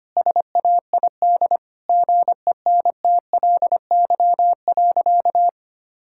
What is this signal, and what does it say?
Morse code, heard as a keyed tone, about 25 wpm, 700 Hz. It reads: SAID GENTLY.
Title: Sentences from Top 1600 Words 25wpm